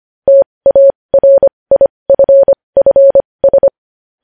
Morse code, heard as a keyed tone, about 25 wpm, spaced as for 20 wpm, 550 Hz. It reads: TARIFFS
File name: podcast / News headlines-1x-25x20